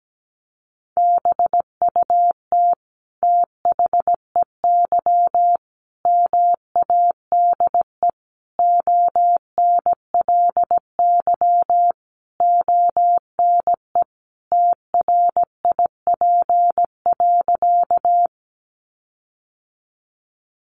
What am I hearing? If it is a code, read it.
Morse code, heard as a keyed tone, about 17 wpm, 700 Hz. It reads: BUT THEY MADE ONLY ONE TRIP.
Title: Sentences from Top 1300 Words 17wpm